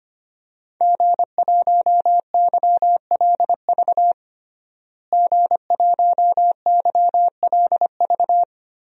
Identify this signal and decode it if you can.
Morse code, heard as a keyed tone, about 25 wpm, 700 Hz. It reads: G1YLV G1YLV
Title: DXCC-100 Call Signs - 5 Characters - Repeated 3x - 25wpm